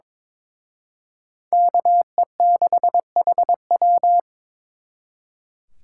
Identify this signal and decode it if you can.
Morse code, heard as a keyed tone, about 26 wpm, 700 Hz. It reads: KE6HW
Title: Speed Racing - USA 2x2 Call Signs 22wpm